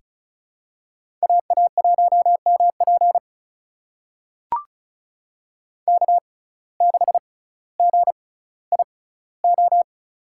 Morse code, heard as a keyed tone, about 36 wpm, spaced as for 18 wpm, 700 Hz. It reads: AA1MP K6GIO
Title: USA Call Signs - 5 Characters - 6x Character Spacing - 35wpm